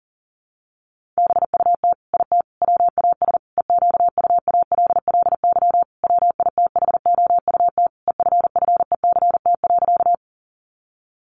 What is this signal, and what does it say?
Morse code, heard as a keyed tone, about 40 wpm, 700 Hz. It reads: BUT IT WAS EQUALLY WITHOUT EFFECT.
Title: Sentences from Top 1400 Words 40wpm